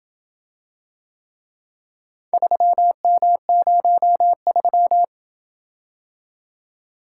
Morse code, heard as a keyed tone, about 26 wpm, 700 Hz. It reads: 3M03